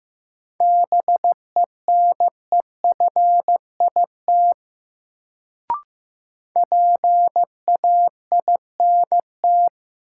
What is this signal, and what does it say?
Morse code, heard as a keyed tone, about 15 wpm, 700 Hz. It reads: BENEFIT PAINT
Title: Sentences from Top 1000 Words - Review Part 6 - New Word Drill - 15wpm